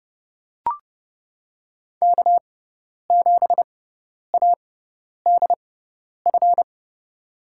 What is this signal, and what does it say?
Morse code, heard as a keyed tone, about 30 wpm, spaced as for 10 wpm, 700 Hz. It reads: K7ADF
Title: USA Call Signs - 6x Character Spacing - 30wpm